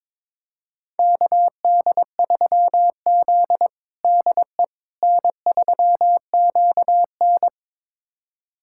Morse code, heard as a keyed tone, about 22 wpm, 700 Hz. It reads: KD3Z DE N3QN
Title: USA Callsign DE Callsign - 4 Characters - 22wpm